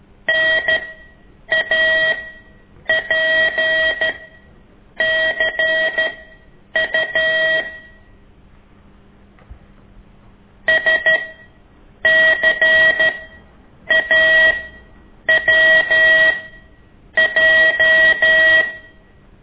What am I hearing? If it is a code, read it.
Morse code, heard as a keyed tone, about 13 wpm, spaced as for 8 wpm, 1900 Hz. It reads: NAPCU SCAWJ
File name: ik1weg2 / CORSI CW